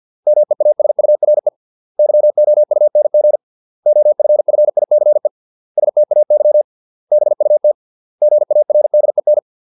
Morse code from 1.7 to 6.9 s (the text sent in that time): YOUNG OFFICE STAY